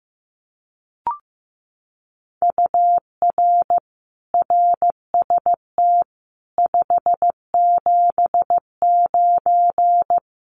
Morse code, heard as a keyed tone, about 15 wpm, 700 Hz. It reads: UR RST 579
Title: Common QSO Phrases Encore 15wpm